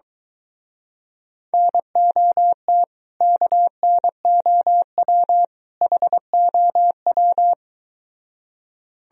Morse code, heard as a keyed tone, about 23 wpm, 700 Hz. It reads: NOT KNOW HOW